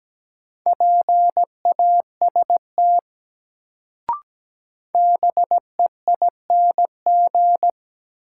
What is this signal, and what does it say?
Morse code, heard as a keyed tone, about 17 wpm, 700 Hz. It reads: PAST BEING
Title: Sentences from Top 600 Words - New Word Drill - 17wpm